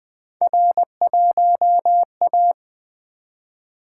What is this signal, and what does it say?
Morse code, heard as a keyed tone, about 20 wpm, 700 Hz. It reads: R1A
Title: Three Letter-Number Combinations - 20wpm